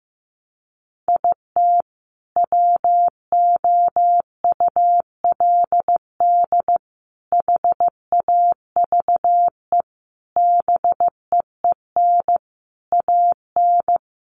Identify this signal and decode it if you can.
Morse code, heard as a keyed tone, about 15 wpm, 700 Hz. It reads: IT WOULD HAVE BEEN AN
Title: Sentences from Top 1100 Words 15wpm